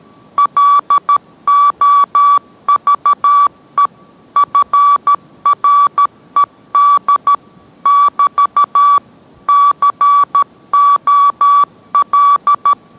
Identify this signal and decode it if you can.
Morse code, heard as a keyed tone, about 13 wpm, 1150 Hz. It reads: LOVE FRED <BT> COL